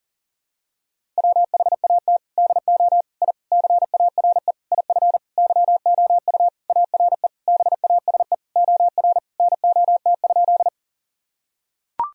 Morse code, heard as a keyed tone, about 40 wpm, 700 Hz. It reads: WHAT DO I CARE IF YOU ARE BASE OR NOT?
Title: Sentences from Top 1000 Words - Review Part 1 - 40wpm